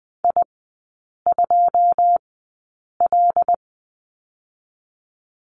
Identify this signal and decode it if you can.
Morse code, heard as a keyed tone, about 20 wpm, spaced as for 8 wpm, 700 Hz. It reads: I2L
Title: 081 Lesson 14e - Number 2 - Three Characters 20wpm